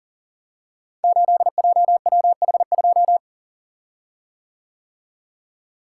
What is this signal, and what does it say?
Morse code, heard as a keyed tone, about 38 wpm, 690 Hz. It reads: 8JWH2
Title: Code Groups - 5 Letter:Numbers - 40wpm